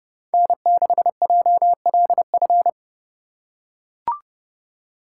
Morse code, heard as a keyed tone, about 31 wpm, 700 Hz. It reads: N6JLF
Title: USA Call Signs - 5 Characters - 3x Character Spacing - 30wpm